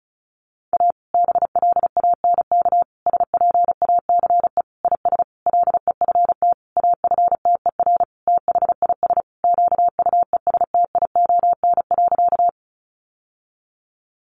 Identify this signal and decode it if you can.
Morse code, heard as a keyed tone, about 34 wpm, 700 Hz. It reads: A BLANK SPACE IS LEFT AFTER THIS QUESTION.
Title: Sentences from Top 1500 Words 35wpm